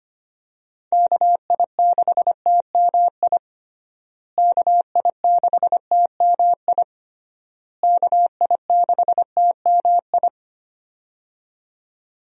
Morse code, heard as a keyed tone, about 25 wpm, 700 Hz. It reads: KI6TMI KI6TMI KI6TMI
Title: USA Call Signs - 6 Characters - Repeated 3x - 25wpm